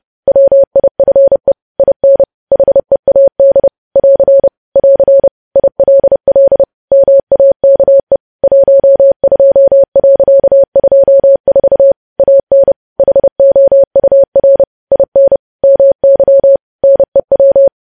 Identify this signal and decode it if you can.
Morse code, heard as a keyed tone, about 30 wpm, 550 Hz. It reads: WIFE IN HEAD <AR> <AR> ILL MAKE 12.24 AN HOUR IN MY NEW
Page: news headlines in morse code at 30 wpm